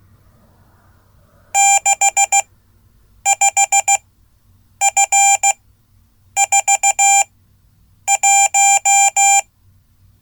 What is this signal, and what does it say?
Morse code, heard as a keyed tone, about 16 wpm, spaced as for 8 wpm, 790 Hz. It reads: 65F41
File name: Code / 5min-a